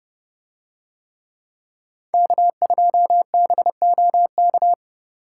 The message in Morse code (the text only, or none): K2BOK